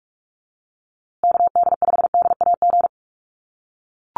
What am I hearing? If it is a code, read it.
Morse code, heard as a keyed tone, about 44 wpm, 700 Hz. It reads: KD5DAG
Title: USA Call Signs Repeated 2x 45wpm